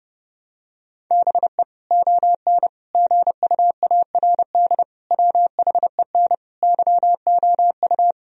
Morse code, heard as a keyed tone, about 30 wpm, 700 Hz. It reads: BE ON GUARD WHEN YOU
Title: Sentences from Top 1000 Words - Review Part 4 - 30wpm